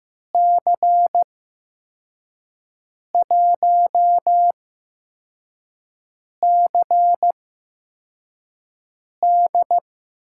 Morse code, heard as a keyed tone, about 15 wpm, spaced as for 4 wpm, 700 Hz. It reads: C1CD
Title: USA Call Signs - 8x Character Spacing - 15wpm